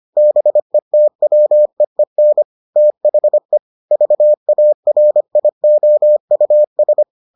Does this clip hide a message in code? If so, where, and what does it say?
Yes: Morse code from 0.2 to 7.0 s, heard as a keyed tone, about 25 wpm, 600 Hz. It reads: BETWEEN THE VARIOUS